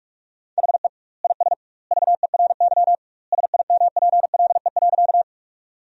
Morse code, heard as a keyed tone, about 42 wpm, 700 Hz. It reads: HE IS VERY SIMPLE.
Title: 2x Word-Spacing - Sentences from Top 400 Words - 45wpm